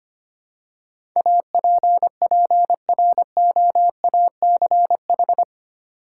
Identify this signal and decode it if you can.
Morse code, heard as a keyed tone, about 25 wpm, 700 Hz. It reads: APPROACH
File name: Eight Letter Words - 25wpm